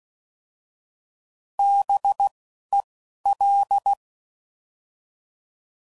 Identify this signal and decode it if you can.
Morse code, heard as a keyed tone, about 16 wpm, 800 Hz. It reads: B E L